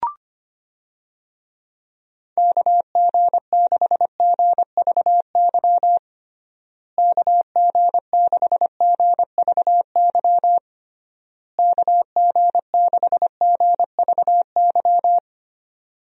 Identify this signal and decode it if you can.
Morse code, heard as a keyed tone, about 25 wpm, 700 Hz. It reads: KG6GVY KG6GVY KG6GVY
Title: USA Call Signs - 6 Characters - Repeated 3x - 25wpm